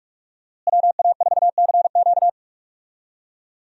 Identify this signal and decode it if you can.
Morse code, heard as a keyed tone, about 47 wpm, 700 Hz. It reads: WA4CQ
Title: USA Call Signs - 5 Characters - Repeated 2x - 45wpm